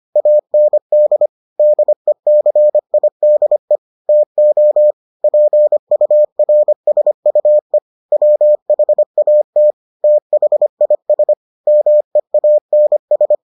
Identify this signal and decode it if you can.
Morse code, heard as a keyed tone, about 25 wpm, 600 Hz. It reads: AND DECIDE TO PURSUE WHAT THIS MEANS